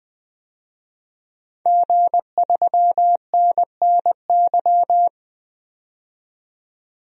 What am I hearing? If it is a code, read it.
Morse code, heard as a keyed tone, about 20 wpm, 700 Hz. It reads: G3NNY
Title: DXCC-100 Call Signs - 5 Characters - 20wpm